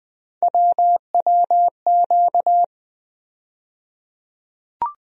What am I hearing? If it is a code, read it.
Morse code, heard as a keyed tone, about 20 wpm, 700 Hz. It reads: WWQ
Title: USA Call Sign Postfixes - 3 Letters 20wpm